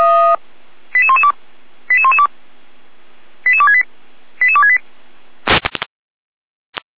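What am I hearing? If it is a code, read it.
Five-tone selective call (ZVEI), heard as quick runs of five tones, tones of about 70 ms. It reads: 88112 88112 88268 88268
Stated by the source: ZVEI five-tone call received by SDR, from Feuerwehr Orlamünde fire brigade audio archive.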